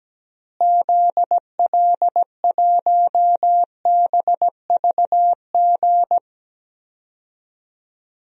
Morse code, heard as a keyed tone, about 17 wpm, 700 Hz. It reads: ZL1BVG
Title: DXCC-100 Call Signs - 6 Characters - Repeated 3x - 17wpm